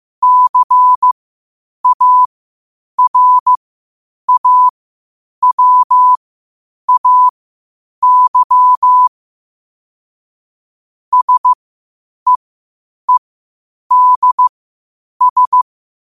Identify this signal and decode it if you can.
Morse code, heard as a keyed tone, about 15 wpm, spaced as for 8 wpm, 1000 Hz. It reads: CARAWAY SEEDS
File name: qotdfiles / 8wpm